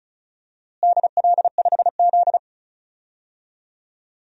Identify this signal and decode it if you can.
Morse code, heard as a keyed tone, about 36 wpm, 700 Hz. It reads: DL5Z
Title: DXCC-100 Call Signs - 4 Characters - Repeated 2x - 35wpm